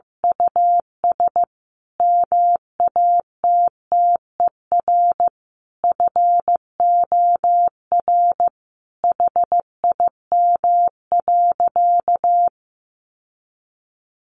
Morse code, heard as a keyed tone, about 15 wpm, 700 Hz. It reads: US MATTER FOR HIM.
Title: Sentences from Top 1100 Words 15wpm